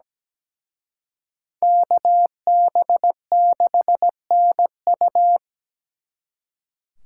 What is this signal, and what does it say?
Morse code, heard as a keyed tone, about 20 wpm, 700 Hz. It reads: KB6NU
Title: Speed Racing - USA 2x2 Call Signs 17wpm